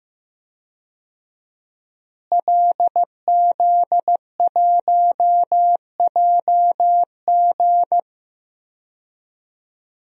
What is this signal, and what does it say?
Morse code, heard as a keyed tone, about 15 wpm, 700 Hz. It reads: LZ1JG
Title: DXCC-100 Call Signs - 5 Characters - Repeated 2x - 15wpm